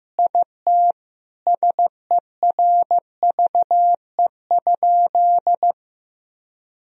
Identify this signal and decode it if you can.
Morse code, heard as a keyed tone, about 15 wpm, 700 Hz. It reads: IT SERVE?
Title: Sentences from Top 1000 Words - Review Part 2 - 15wpm